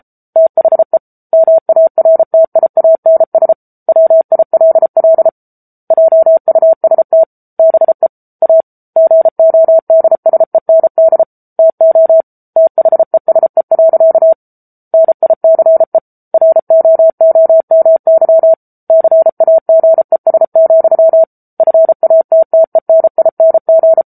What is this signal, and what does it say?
Morse code, heard as a keyed tone, about 34 wpm, 650 Hz. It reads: THE MARTIANS WILL JUST BE A GODSEND TO THESE. NICE ROOMY CAGES, FATTENING
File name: Book Two / 07-Chapter 7 - 33 WPM